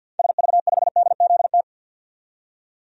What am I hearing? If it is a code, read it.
Morse code, heard as a keyed tone, about 52 wpm, 700 Hz. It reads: SV5DZT
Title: DXCC-100 Call Signs - 6 Characters - 50wpm